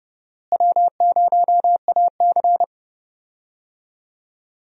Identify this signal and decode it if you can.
Morse code, heard as a keyed tone, about 30 wpm, 700 Hz. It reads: W0AC
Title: USA Call Signs - 4 Characters - Repeated 2x - 30wpm